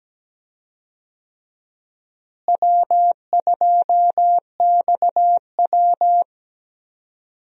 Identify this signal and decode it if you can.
Morse code, heard as a keyed tone, about 17 wpm, 700 Hz. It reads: W2XW
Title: USA Call Signs - 4 Characters - Repeated 3x - 17wpm